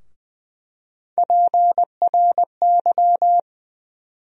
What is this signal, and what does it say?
Morse code, heard as a keyed tone, about 20 wpm, 700 Hz. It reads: PRY
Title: USA Call Sign Postfixes - 3 Letters 20wpm